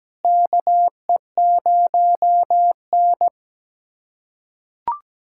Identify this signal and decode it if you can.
Morse code, heard as a keyed tone, about 17 wpm, 700 Hz. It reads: KE0N
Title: USA Callsign DE Callsign - 4 Characters - 17wpm